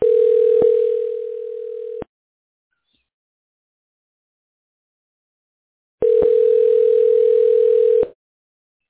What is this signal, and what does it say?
Ringback tone (North America), 440 + 480 Hz.